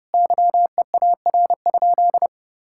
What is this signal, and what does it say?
Morse code, heard as a keyed tone, about 30 wpm, 700 Hz. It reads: YEAR?
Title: Sentences from Top 1000 Words - Review Part 5 - 30wpm